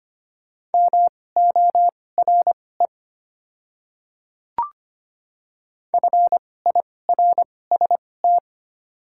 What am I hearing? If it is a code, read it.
Morse code, heard as a keyed tone, about 25 wpm, spaced as for 18 wpm, 700 Hz. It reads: MORE FIRST
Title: 2x Character Spacing - Top 200 Words - 25wpm